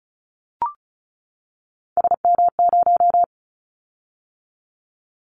Morse code, heard as a keyed tone, about 36 wpm, 700 Hz. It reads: SM0
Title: Callsign Prefixes for Top 25 DXCC - 35wpm